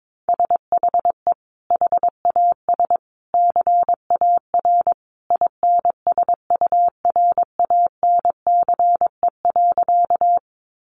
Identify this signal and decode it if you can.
Morse code, heard as a keyed tone, about 22 wpm, 700 Hz. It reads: SHE HAS CAR INSURANCE.
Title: Sentences from Top 1100 Words 22wpm